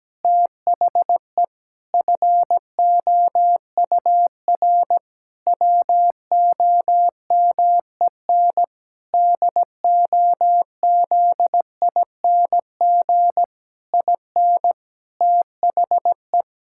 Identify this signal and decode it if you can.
Morse code, heard as a keyed tone, about 17 wpm, 700 Hz. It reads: THE FOUR WOMEN DOZING IN THE
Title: Pangram Challenge v2 - 17wpm